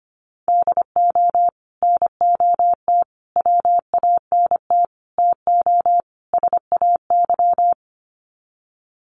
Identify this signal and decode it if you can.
Morse code, heard as a keyed tone, about 25 wpm, 700 Hz. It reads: DO NOT WANT TO SAY
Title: Sets of 5 Words Encore 25wpm